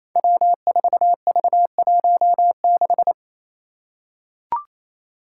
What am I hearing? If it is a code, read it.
Morse code, heard as a keyed tone, about 28 wpm, 700 Hz. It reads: W4V16
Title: Five Challenging Letter-Number Combinations 28wpm